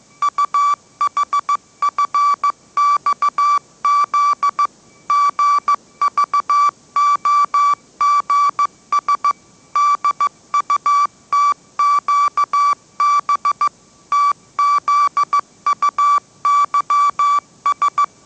Morse code, heard as a keyed tone, about 15 wpm, 1160 Hz. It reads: UHFXZ GVOGS DUTQB TZUYS